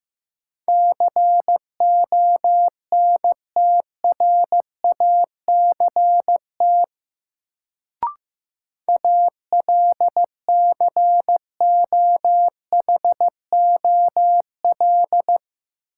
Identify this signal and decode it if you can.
Morse code, heard as a keyed tone, about 15 wpm, 700 Hz. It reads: CONTRACT ALCOHOL
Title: Sentences from Top 1400 Words - New Word Drill - 15wpm